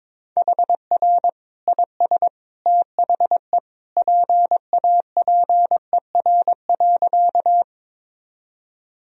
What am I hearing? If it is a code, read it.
Morse code, heard as a keyed tone, about 22 wpm, 700 Hz. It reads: HR IS THE PAPER.
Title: Sentences from Top 300 Words and Abbreviations 22wpm